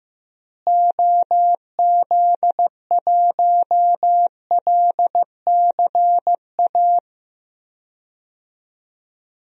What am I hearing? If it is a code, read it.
Morse code, heard as a keyed tone, about 15 wpm, 700 Hz. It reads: OZ1LCA
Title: DXCC-100 Call Signs - 6 Characters - Repeated 2x - 15wpm